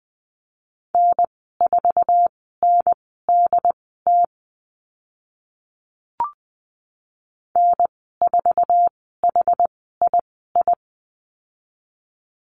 Morse code, heard as a keyed tone, about 20 wpm, spaced as for 14 wpm, 700 Hz. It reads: N4NDT N4HII
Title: 069 Lesson 12g - Letter d - Call Signs 20wpm